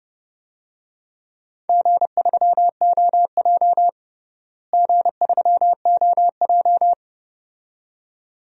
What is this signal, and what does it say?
Morse code, heard as a keyed tone, about 30 wpm, 700 Hz. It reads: G3OJ G3OJ
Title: DXCC-100 Call Signs - 4 Characters - Repeated 2x - 30wpm